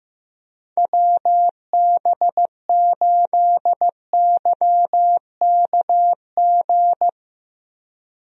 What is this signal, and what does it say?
Morse code, heard as a keyed tone, about 16 wpm, 700 Hz. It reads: WB8YKG
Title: USA Call Signs - 6 Characters - Repeated 2x - 15wpm